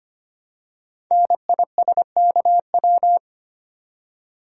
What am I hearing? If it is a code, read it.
Morse code, heard as a keyed tone, about 25 wpm, 700 Hz. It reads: NISKW
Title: Code Groups - 5 Letters - 25wpm